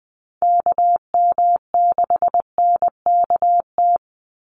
Morse code, heard as a keyed tone, about 20 wpm, 700 Hz. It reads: KM6NKT
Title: USA Call Signs 20wpm